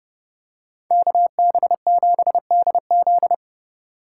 Morse code, heard as a keyed tone, about 28 wpm, 700 Hz. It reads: KB7DZ